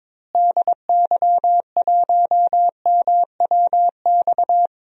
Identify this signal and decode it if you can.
Morse code, heard as a keyed tone, about 22 wpm, 700 Hz. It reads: DY1MWX
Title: DXCC-100 Call Signs - 6 Characters - Repeated 3x - 22wpm